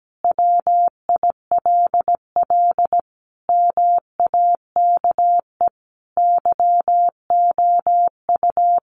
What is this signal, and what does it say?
Morse code, heard as a keyed tone, about 17 wpm, 700 Hz. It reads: WILL MAKE YOU
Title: Sentences from Top 1500 Words 17wpm